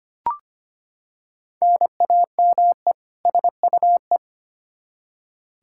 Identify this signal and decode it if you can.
Morse code, heard as a keyed tone, about 25 wpm, 700 Hz. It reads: NAME SUE
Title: Practice QSO - Exchange 1 - Beginning - 25wpm